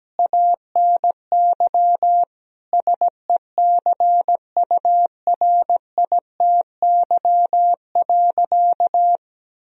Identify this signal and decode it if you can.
Morse code, heard as a keyed tone, about 17 wpm, 700 Hz. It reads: ANY SECURITY.
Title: Sentences from Top 1000 Words - Review Part 4 - 17wpm